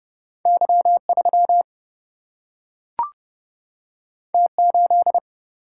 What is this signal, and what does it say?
Morse code, heard as a keyed tone, about 30 wpm, 700 Hz. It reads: Y 3 T 8
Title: Single-Letter Single-Number 30wpm